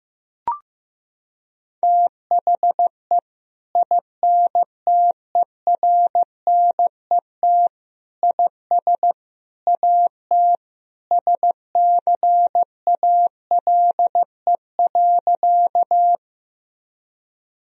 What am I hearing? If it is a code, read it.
Morse code, heard as a keyed tone, about 15 wpm, 700 Hz. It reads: THE INTERNET IS AT SCALE.
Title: Sentences from Top 1000 Words - Review Part 4 - 15wpm